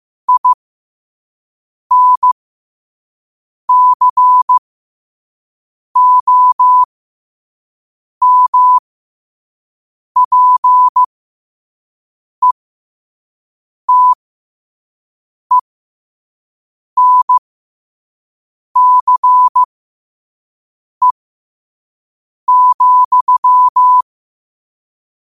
Morse code, heard as a keyed tone, about 15 wpm, spaced as for 5 wpm, 1000 Hz. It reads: INCOMPETENCE,
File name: qotdfiles / 5wpm